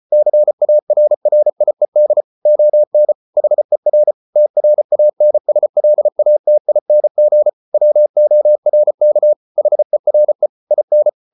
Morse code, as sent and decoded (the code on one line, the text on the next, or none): -.-. .- .-. .-. .. -.. / --- -. / .... .-. / - .-. .- -. ... .-.. .- - .. -. --. / .-- --- .-. -.- / .... .-. / .. -.
CARRIED ON HER TRANSLATING WORK HERE IN